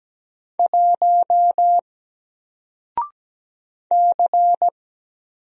Morse code, heard as a keyed tone, about 17 wpm, 700 Hz. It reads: 1 C
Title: Single Letter-Number 17wpm